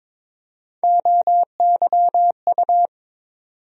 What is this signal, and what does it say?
Morse code, heard as a keyed tone, about 22 wpm, 700 Hz. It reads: OYU
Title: USA Call Sign Postfixes - 3 Letters 22wpm